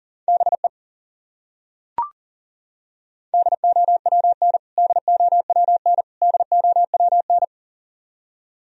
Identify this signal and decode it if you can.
Morse code, heard as a keyed tone, about 37 wpm, 700 Hz. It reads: BE DOWN DOWN DOWN
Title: Top 100 Words Repeated 40wpm